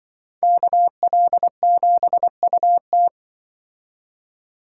Morse code, heard as a keyed tone, about 24 wpm, 700 Hz. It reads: KL7UT
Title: Speed Racing - USA 2x2 Call Signs 20wpm